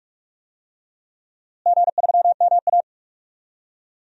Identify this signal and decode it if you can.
Morse code, heard as a keyed tone, about 46 wpm, 700 Hz. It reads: G3MA
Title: DXCC-100 Call Signs - 4 Characters - Repeated 2x - 45wpm